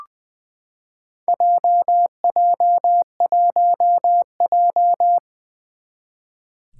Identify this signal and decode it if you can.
Morse code, heard as a keyed tone, about 20 wpm, 700 Hz. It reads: JJ1J
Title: Four Challenging Letter-Number Combinations 20wpm